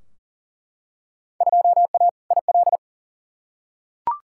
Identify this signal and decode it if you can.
Morse code, heard as a keyed tone, about 40 wpm, 700 Hz. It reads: 2A IL
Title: Field Day Exchanges - 40wpm